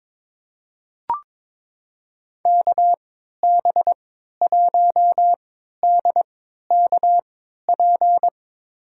Morse code, heard as a keyed tone, about 21 wpm, spaced as for 12 wpm, 700 Hz. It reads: KB1DKP
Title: USA Call Signs - 6 Characters - 3x Character Spacing - 22wpm